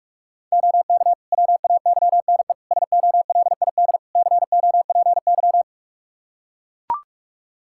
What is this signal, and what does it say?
Morse code, heard as a keyed tone, about 44 wpm, 700 Hz. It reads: OK WAYNE SOLID COPY
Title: Practice QSO - Exchange 3 - More Info - 45wpm